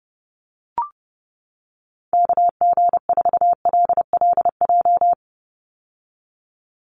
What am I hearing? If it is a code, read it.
Morse code, heard as a keyed tone, about 29 wpm, 700 Hz. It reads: KG4LLJ
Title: USA Call Signs 30wpm